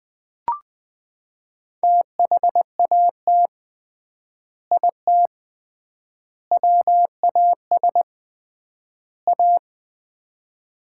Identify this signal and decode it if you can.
Morse code, heard as a keyed tone, about 20 wpm, 700 Hz. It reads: THAT IT WAS A
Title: 3x Word-Spacing - Sets of 4 Words - 20wpm